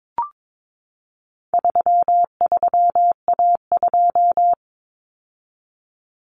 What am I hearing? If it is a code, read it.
Morse code, heard as a keyed tone, about 22 wpm, 700 Hz. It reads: 33A2
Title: Common Cut Numbers - 4-Digit 22wpm